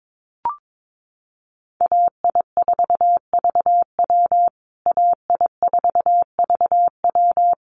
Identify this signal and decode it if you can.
Morse code, heard as a keyed tone, about 22 wpm, 700 Hz. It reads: AI4VW AI4VW
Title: USA Call Signs Repeated 3x 22wpm